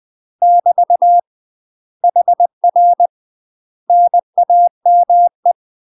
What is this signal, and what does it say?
Morse code, heard as a keyed tone, about 20 wpm, 700 Hz. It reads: <BT> HR NAME
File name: UK MASTER QSOs TRAINING/20WPM